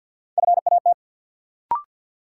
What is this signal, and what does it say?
Morse code, heard as a keyed tone, about 52 wpm, 700 Hz. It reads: FAT
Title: Sentences from Top 600 Words - New Word Drill - 50wpm